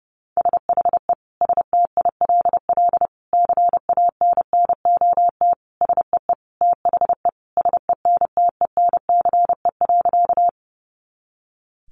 Morse code, heard as a keyed tone, about 28 wpm, 700 Hz. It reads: SHE STILL CANNOT SEE THE SENTENCE.
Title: Additional Easy Phrases 30wpm